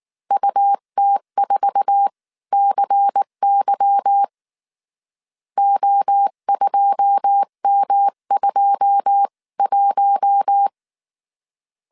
Morse code, heard as a keyed tone, about 19 wpm, spaced as for 16 wpm, 800 Hz. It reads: UT4CY O2M21
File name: CW / DET / 1 9 1500C S C 18wpm 2539